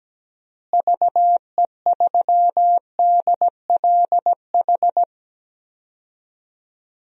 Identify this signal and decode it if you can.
Morse code, heard as a keyed tone, about 17 wpm, 700 Hz. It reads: VE3DLH
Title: DXCC-100 Call Signs - 6 Characters - Repeated 2x - 17wpm